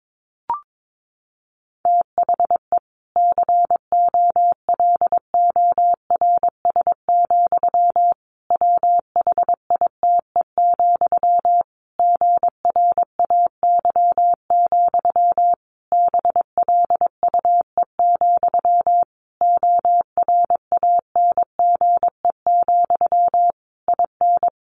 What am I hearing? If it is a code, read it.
Morse code, heard as a keyed tone, about 22 wpm, 700 Hz. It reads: THE COLORS, WHITE, GRAY, BLUE, ORANGE, IN
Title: A Sound Of Thunder 22wpm